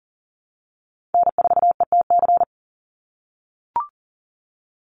Morse code, heard as a keyed tone, about 41 wpm, 700 Hz. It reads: N4ETC
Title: Call Signs - 40wpm